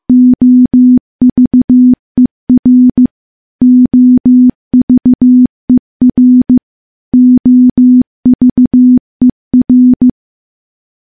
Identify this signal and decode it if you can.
Morse code, heard as a keyed tone, about 15 wpm, 250 Hz. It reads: OVER OVER OVER
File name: morse-training-15WPM